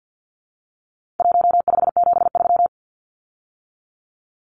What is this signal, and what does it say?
Morse code, heard as a keyed tone, about 50 wpm, 700 Hz. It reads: 1573
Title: Numbers - Contest Style - 4-Digit 50wpm